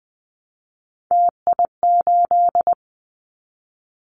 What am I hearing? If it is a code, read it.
Morse code, heard as a keyed tone, about 20 wpm, 700 Hz. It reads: TI8
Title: Callsign Prefixes for Top 100 DXCC - 20wpm